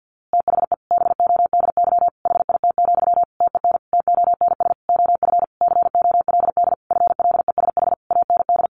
Morse code, heard as a keyed tone, about 50 wpm, 700 Hz. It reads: THE BODY HIT, TEN TONS OF COLD FLESH AND
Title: A Sound Of Thunder 50wpm